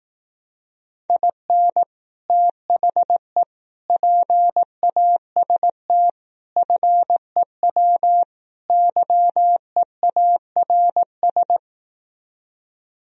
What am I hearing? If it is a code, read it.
Morse code, heard as a keyed tone, about 18 wpm, 700 Hz. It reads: IN THE PAST FEW YEARS
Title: Speed-Racing - Sets of 5 Words 15wpm